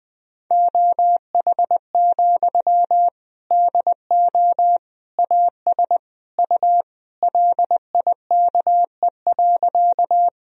Morse code, heard as a keyed tone, about 20 wpm, 700 Hz. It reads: OH, DO AS U LIKE.
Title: Sentences from Top 500 Words and Abbreviations 20wpm